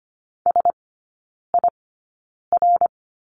Morse code, heard as a keyed tone, about 25 wpm, spaced as for 9 wpm, 700 Hz. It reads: SIR
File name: course / 25:10wpm / 093 Lesson 16e - Letter c - Three Characters 25wpm